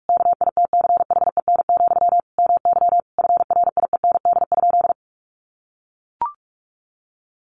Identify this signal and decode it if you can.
Morse code, heard as a keyed tone, about 44 wpm, 700 Hz. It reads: KITCHEN, MY FRIEND?
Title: Sentences from Top 1400 Words 45wpm